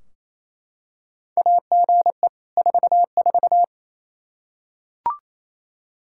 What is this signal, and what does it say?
Morse code, heard as a keyed tone, about 27 wpm, 700 Hz. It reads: AGE 44
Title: Practice QSO - Exchange 3 - More Info - 28wpm